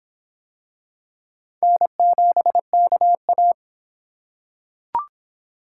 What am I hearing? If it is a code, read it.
Morse code, heard as a keyed tone, about 24 wpm, 700 Hz. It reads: N7KA